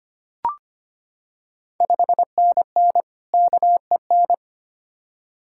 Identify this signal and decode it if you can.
Morse code, heard as a keyed tone, about 25 wpm, 700 Hz. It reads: 5NN KEN
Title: Ontario QSO Party Exchange 25wpm